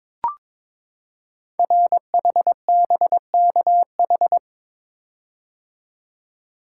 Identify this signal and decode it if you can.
Morse code, heard as a keyed tone, about 23 wpm, 700 Hz. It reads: RHBKH
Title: Code Groups - 5 Letters - 22wpm